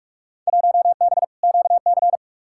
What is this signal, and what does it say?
Morse code, heard as a keyed tone, about 46 wpm, 700 Hz. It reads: 1B QC
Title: Field Day Exchanges - 45wpm